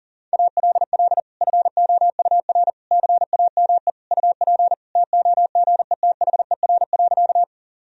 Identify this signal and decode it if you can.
Morse code, heard as a keyed tone, about 37 wpm, 700 Hz. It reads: ALL FOUR CAME UP TOGETHER.
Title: Sentences from Top 1000 Words - Review Part 1 - 40wpm